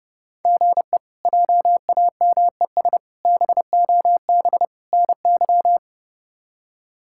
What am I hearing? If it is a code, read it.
Morse code, heard as a keyed tone, about 30 wpm, 700 Hz. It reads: GE JAMES BOB NY
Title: K1USN SST QSO Practice - 30wpm